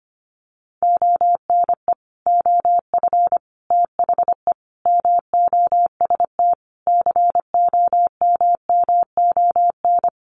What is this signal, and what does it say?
Morse code, heard as a keyed tone, about 25 wpm, 700 Hz. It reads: ONE OF THE MOST COMMON
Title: Sets of 5 Words Encore2 25wpm